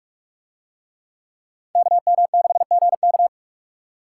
Morse code, heard as a keyed tone, about 44 wpm, 700 Hz. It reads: KM6GK